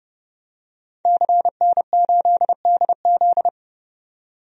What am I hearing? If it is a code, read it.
Morse code, heard as a keyed tone, about 30 wpm, 700 Hz. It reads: CN8DZ
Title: DXCC-100 Call Signs - 5 Characters - 30wpm